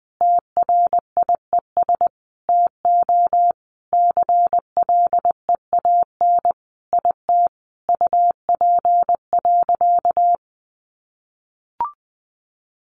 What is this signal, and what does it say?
Morse code, heard as a keyed tone, about 20 wpm, 700 Hz. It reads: TRIES TO CLEAN IT UP.
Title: Additional Easy Phrases 20wpm